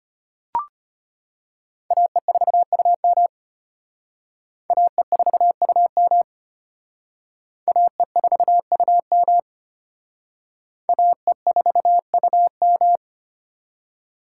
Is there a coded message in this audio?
Yes: Morse code from 1.9 to 13.0 s, heard as a keyed tone, about 32 wpm, 700 Hz. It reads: AE4UM AE4UM AE4UM AE4UM